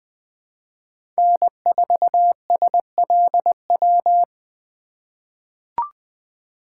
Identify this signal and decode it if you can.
Morse code, heard as a keyed tone, about 20 wpm, 700 Hz. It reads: N4SLW
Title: USA Call Signs - 5 Characters - 4x Character Spacing - 20wpm